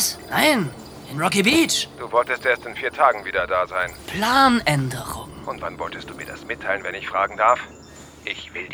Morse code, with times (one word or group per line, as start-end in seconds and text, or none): none